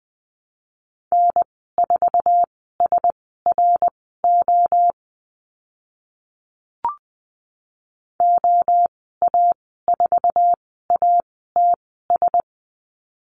Morse code, heard as a keyed tone, about 20 wpm, spaced as for 14 wpm, 700 Hz. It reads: N4SRO OA4ATS
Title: 055 Lesson 10g - Letter r - Call Signs 20wpm